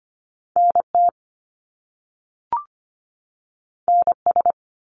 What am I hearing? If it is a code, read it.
Morse code, heard as a keyed tone, about 25 wpm, 700 Hz. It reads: NT NS
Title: Canadian Provinces and Territories 25wpm